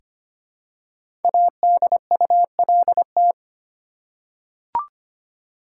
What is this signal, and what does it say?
Morse code, heard as a keyed tone, about 25 wpm, 700 Hz. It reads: ADULT
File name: 5 Letter Words - First 3 Letters Given - 25wpm